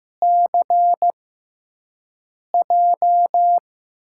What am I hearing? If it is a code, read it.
Morse code, heard as a keyed tone, about 16 wpm, spaced as for 5 wpm, 700 Hz. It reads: CJ
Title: USA Call Signs - 6x Character Spacing - 15wpm